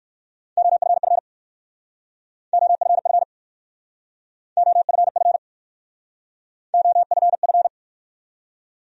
Morse code, heard as a keyed tone, about 54 wpm, 700 Hz. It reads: OFF OFF OFF OFF
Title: Speed-Racing Top 300 Words 45wpm